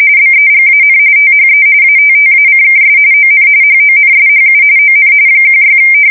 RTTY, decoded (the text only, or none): BEGAN THE ATTACKS ON HONSHU ON 16